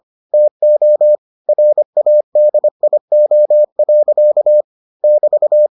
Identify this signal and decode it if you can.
Morse code, heard as a keyed tone, about 25 wpm, 600 Hz. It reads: TO RADIO. <BT>